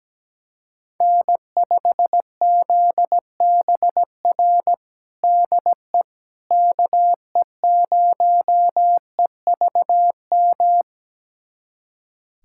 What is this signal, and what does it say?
Morse code, heard as a keyed tone, about 17 wpm, 700 Hz. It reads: N5ZBR DE KE0EVM
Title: USA Callsign DE Callsign 17wpm